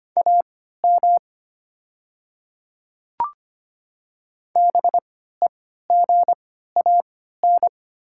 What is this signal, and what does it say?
Morse code, heard as a keyed tone, about 25 wpm, spaced as for 14 wpm, 690 Hz. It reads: AM BEGAN
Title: 3x Character Spacing - Top 500 Words - 25wpm